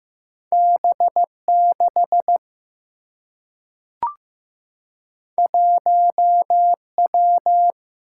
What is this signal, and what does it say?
Morse code, heard as a keyed tone, about 15 wpm, 700 Hz. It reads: B6 1W
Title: Single-Letter Single-Number 15wpm